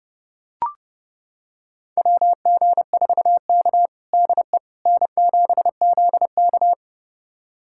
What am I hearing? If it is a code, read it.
Morse code, heard as a keyed tone, about 30 wpm, 700 Hz. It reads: WG4K DE N7ZK
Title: USA Callsign DE Callsign - 4 Characters - 30wpm